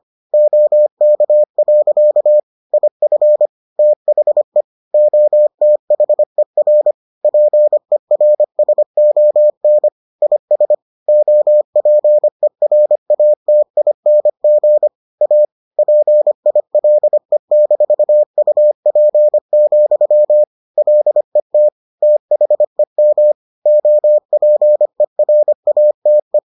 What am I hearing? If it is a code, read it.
Morse code, heard as a keyed tone, about 25 wpm, 600 Hz. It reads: OK. IF THE OTHER PERSON IS OPERATING A PILE-UP, LET THEM OPERATE